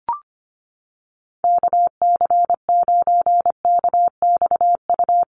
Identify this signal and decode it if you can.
Morse code, heard as a keyed tone, about 25 wpm, 700 Hz. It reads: KC9KXU
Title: USA Call Signs Repeated 3x 25wpm